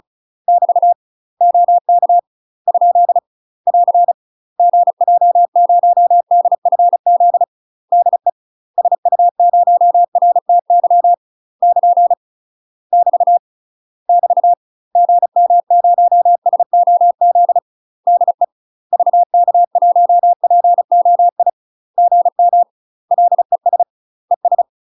Morse code, heard as a keyed tone, about 34 wpm, 700 Hz. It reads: <BT> OK ? <AR> GJ0DFZ DE SU0RTY <KN> <BT> <BT> GM0SOZ DE VK1POI GM LES ES
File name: UK MASTER QSOs TRAINING / 35WPM / UK MASTER QSOs 0025